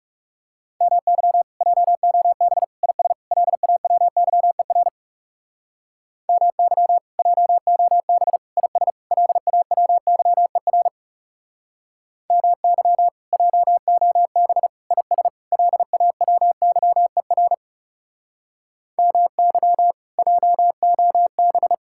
Morse code, heard as a keyed tone, about 35 wpm, 700 Hz. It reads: MY JOB IS LAWYER MY JOB IS LAWYER MY JOB IS LAWYER MY JOB
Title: Speed-Racing Common QSO Phrases - 30wpm